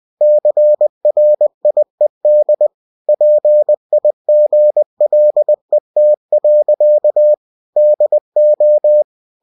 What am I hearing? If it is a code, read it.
Morse code, heard as a keyed tone, about 20 wpm, 600 Hz. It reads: CRIED PIGLET. DO